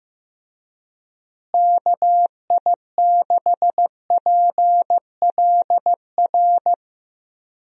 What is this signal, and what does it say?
Morse code, heard as a keyed tone, about 16 wpm, 700 Hz. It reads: KI6PLR
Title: USA Call Signs - 6 Characters - Repeated 2x - 15wpm